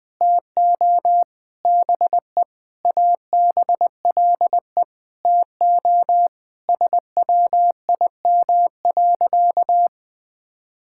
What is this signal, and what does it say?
Morse code, heard as a keyed tone, about 20 wpm, 700 Hz. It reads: TO BE ABLE TO SWIM.
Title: Sentences with 5-Word Phrases - 20wpm